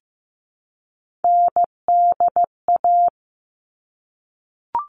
Morse code, heard as a keyed tone, about 15 wpm, 700 Hz. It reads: NDA
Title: All Cut Numbers - 3-Digit 15wpm